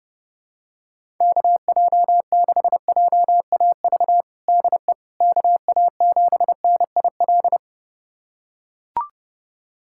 Morse code, heard as a keyed tone, about 30 wpm, 700 Hz. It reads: KJ6JAV DE KA7NIL
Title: USA Callsign DE Callsign 30wpm